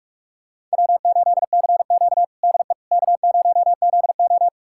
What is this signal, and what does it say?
Morse code, heard as a keyed tone, about 41 wpm, 700 Hz. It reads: W8CQ DE K0ZO
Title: USA Callsign DE Callsign - 4 Characters - 45wpm